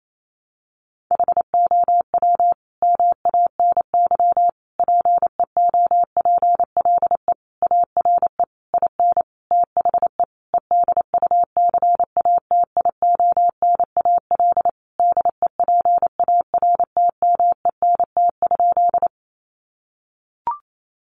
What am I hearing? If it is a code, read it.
Morse code, heard as a keyed tone, about 28 wpm, 700 Hz. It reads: HOW MANY PEOPLE ARE IN THE EDUCATIONAL DEPARTMENT?
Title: Sentences from Top 1200 Words 28wpm